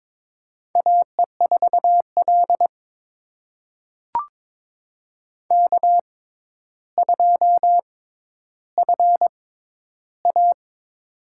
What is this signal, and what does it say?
Morse code, heard as a keyed tone, about 22 wpm, spaced as for 11 wpm, 700 Hz. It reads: AE4L K2FA
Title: USA Call Signs - 6x Character Spacing - 22wpm